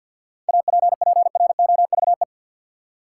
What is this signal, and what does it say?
Morse code, heard as a keyed tone, about 52 wpm, 700 Hz. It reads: APPROVE
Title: Seven Letter Words - 50wpm